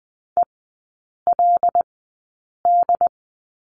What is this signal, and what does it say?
Morse code, heard as a keyed tone, about 20 wpm, spaced as for 8 wpm, 700 Hz. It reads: ELD
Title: 166 Lesson 27e - Number 7 - Three Characters 20wpm